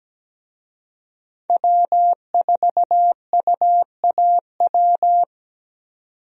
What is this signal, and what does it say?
Morse code, heard as a keyed tone, about 17 wpm, 700 Hz. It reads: W4UAW
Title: USA Call Signs - 5 Characters - 17wpm